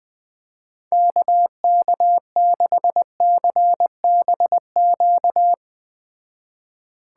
Morse code, heard as a keyed tone, about 20 wpm, 700 Hz. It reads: KK6CBQ
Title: Speed Racing - USA 2x3 Call Signs 17wpm